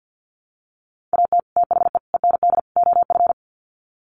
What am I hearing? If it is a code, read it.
Morse code, heard as a keyed tone, about 50 wpm, 700 Hz. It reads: AT THE END OF
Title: Sets of 4 Words 50wpm